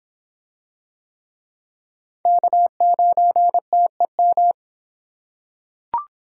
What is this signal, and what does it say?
Morse code, heard as a keyed tone, about 26 wpm, 700 Hz. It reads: K9TEM